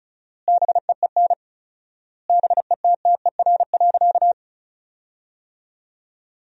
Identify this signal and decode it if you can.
Morse code, heard as a keyed tone, about 34 wpm, 700 Hz. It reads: BEEN BETTER.
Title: 4x Word-Spacing - Sentences from Top 500 Words - 35wpm